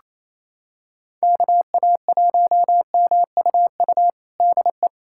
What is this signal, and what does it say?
Morse code, heard as a keyed tone, about 28 wpm, 700 Hz. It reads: KA1MUU DE
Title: USA Callsign DE Callsign - 6 Characters - 28wpm